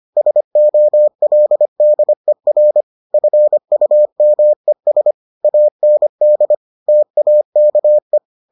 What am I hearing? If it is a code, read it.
Morse code, heard as a keyed tone, about 25 wpm, 600 Hz. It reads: SOLDER FUMES AND TAKE